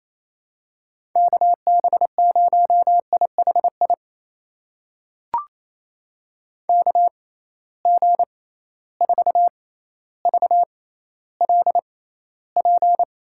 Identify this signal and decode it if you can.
Morse code, heard as a keyed tone, about 28 wpm, spaced as for 14 wpm, 700 Hz. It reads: KB0IHI KG4VLP
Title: USA Call Signs - 6 Characters - 6x Character Spacing - 28wpm